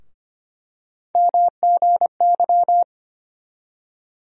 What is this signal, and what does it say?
Morse code, heard as a keyed tone, about 25 wpm, 700 Hz. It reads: MGY
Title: USA Call Sign Postfixes - 3 Letters 25wpm